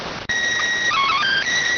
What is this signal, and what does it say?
SSTV (slow-scan television), PD 290.